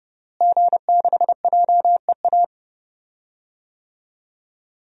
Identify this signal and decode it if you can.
Morse code, heard as a keyed tone, about 29 wpm, 700 Hz. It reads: G6JEA
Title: DXCC-100 Call Signs - 5 Characters - Repeated 3x - 30wpm